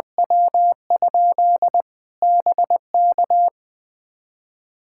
Morse code, heard as a keyed tone, about 20 wpm, 700 Hz. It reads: W? BK
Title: Practice QSO - Exchange 2 - Info - 20wpm